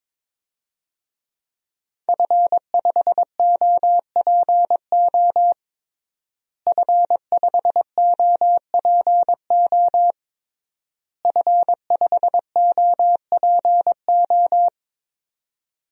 Morse code, heard as a keyed tone, about 22 wpm, 700 Hz. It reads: F5OPO F5OPO F5OPO
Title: DXCC-100 Call Signs - 5 Characters - Repeated 3x - 22wpm